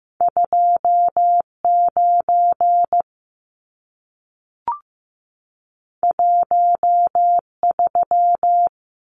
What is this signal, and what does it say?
Morse code, heard as a keyed tone, about 15 wpm, 700 Hz. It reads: 29 13
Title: Numbers - 2-Digit 15wpm